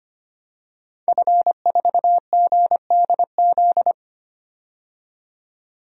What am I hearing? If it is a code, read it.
Morse code, heard as a keyed tone, about 25 wpm, 700 Hz. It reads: F4GDZ
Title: DXCC-100 Call Signs - 5 Characters - 25wpm